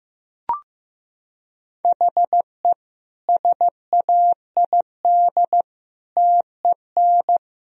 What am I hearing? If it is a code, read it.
Morse code, heard as a keyed tone, about 15 wpm, 700 Hz. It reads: HE SAID TEN
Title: Sentences from Top 1000 Words - Review Part 3 - 15wpm